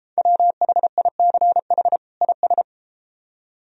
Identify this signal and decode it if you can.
Morse code, heard as a keyed tone, about 33 wpm, 700 Hz. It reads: WHICH IS